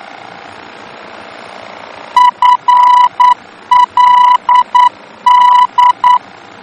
Morse code, heard as a keyed tone, about 9 wpm, 990 Hz. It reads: FLD
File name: germany / mx / FLD-D-VOR